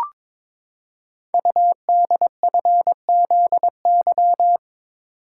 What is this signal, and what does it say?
Morse code, heard as a keyed tone, about 22 wpm, 700 Hz. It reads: UDFZY
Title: Code Groups - 5 Letters - 22wpm